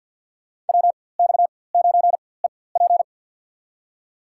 Morse code, heard as a keyed tone, about 46 wpm, spaced as for 23 wpm, 700 Hz. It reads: WX9EP
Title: USA Call Signs - 5 Characters - 4x Character Spacing - 50wpm